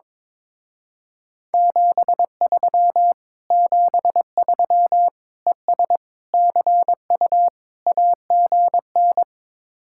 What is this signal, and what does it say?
Morse code, heard as a keyed tone, about 22 wpm, 700 Hz. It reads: 73 73 ES CU AGN
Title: Speed-Racing Common QSO Phrases - 22wpm